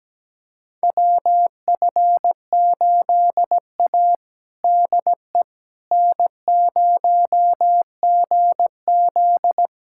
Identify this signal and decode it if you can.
Morse code, heard as a keyed tone, about 17 wpm, 700 Hz. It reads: WF8A DE N0GZ
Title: USA Callsign DE Callsign - 4 Characters - 17wpm